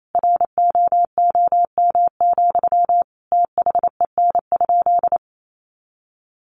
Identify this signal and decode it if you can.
Morse code, heard as a keyed tone, about 28 wpm, 700 Hz. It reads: ROOM, THEN?
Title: Sentences from Top 600 Words 28wpm